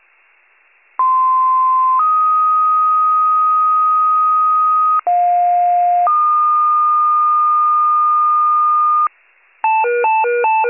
Two-tone page, heard as a one-second tone, then a three-second tone, about 1040 then 1250 Hz, about 710 then 1220 Hz. It is heard from 1.0 to 9.1 s.